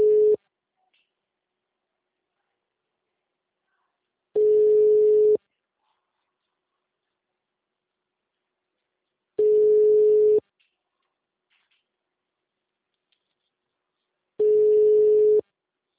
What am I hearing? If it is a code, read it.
Ringback tone (Europe, 425 Hz).